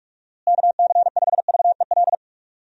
Morse code, heard as a keyed tone, about 44 wpm, 700 Hz. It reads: KC5VEL
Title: USA Call Signs - 6 Characters - Repeated 3x - 45wpm